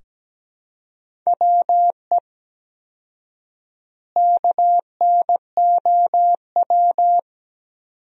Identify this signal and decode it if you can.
Morse code, heard as a keyed tone, about 17 wpm, 700 Hz. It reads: WE KNOW
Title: 4x Word-Spacing - Sentences from Top 500 Words - 17wpm